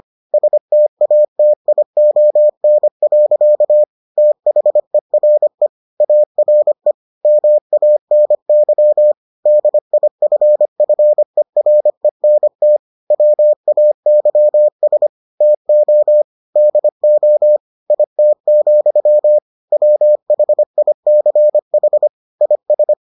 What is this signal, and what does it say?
Morse code, heard as a keyed tone, about 25 wpm, 600 Hz. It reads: STATION. THERE ARE MANY DIFFERENT WAYS TO DO IT, WHICH IS